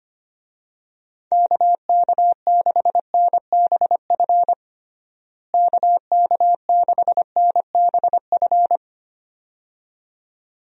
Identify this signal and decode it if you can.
Morse code, heard as a keyed tone, about 25 wpm, 700 Hz. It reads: KK6NBF KK6NBF
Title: USA Call Signs - 6 Characters - Repeated 2x - 25wpm